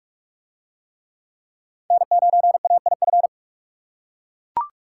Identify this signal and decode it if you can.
Morse code, heard as a keyed tone, about 40 wpm, 700 Hz. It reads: N9AIF